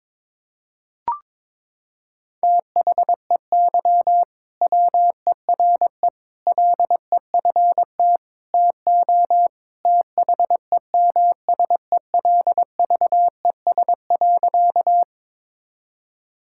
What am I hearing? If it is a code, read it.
Morse code, heard as a keyed tone, about 21 wpm, 700 Hz. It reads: THEY WERE LEFT TO THEMSELVES.
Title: Sentences from Top 1000 Words - Review Part 5 - 22wpm